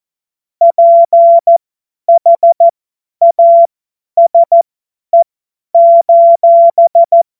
Morse code, heard as a keyed tone, about 14 wpm, spaced as for 10 wpm, 680 Hz. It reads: PHASE:
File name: Morsekurs / Witze / Witz 09. Dezember 2025 50 F70